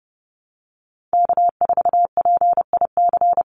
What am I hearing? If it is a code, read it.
Morse code, heard as a keyed tone, about 30 wpm, 700 Hz. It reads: K4PIC
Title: NA SOTA Chaser Callsigns 30wpm